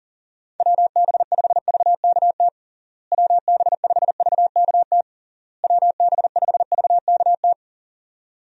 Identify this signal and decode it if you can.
Morse code, heard as a keyed tone, about 37 wpm, 700 Hz. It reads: WB5VKT WB5VKT WB5VKT
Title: USA Call Signs - 6 Characters - Repeated 3x - 40wpm